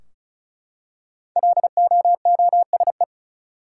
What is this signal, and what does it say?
Morse code, heard as a keyed tone, about 36 wpm, 700 Hz. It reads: LOOSE
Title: Five Letter Words - 35wpm